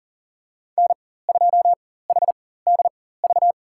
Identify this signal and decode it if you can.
Morse code, heard as a keyed tone, about 40 wpm, spaced as for 19 wpm, 700 Hz. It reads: N2HDV
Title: USA Call Signs - 5 Characters - 4x Character Spacing - 40wpm